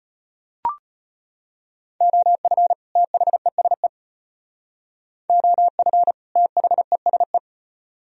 Morse code, heard as a keyed tone, about 33 wpm, 700 Hz. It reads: OF THESE OF THESE